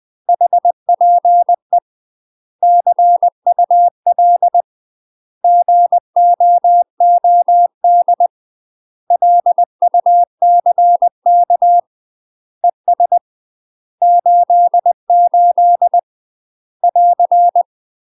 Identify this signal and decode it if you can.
Morse code, heard as a keyed tone, about 20 wpm, 700 Hz. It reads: HPE CUL GOOD LUCK ES 88 <AR>
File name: UK MASTER QSOs TRAINING / 20WPM / UK MASTER QSOs 0040